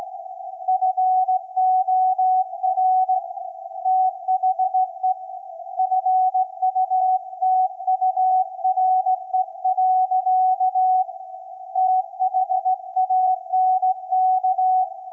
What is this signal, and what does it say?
Morse code, heard as a keyed tone, about 17 wpm, 750 Hz. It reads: FOR THE FUTURE. THANK
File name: iz7slz / VLF / SAQ 24-12-2024